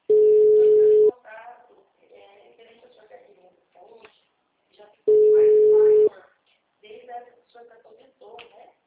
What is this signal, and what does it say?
Ringback tone (Europe, 425 Hz), with speech.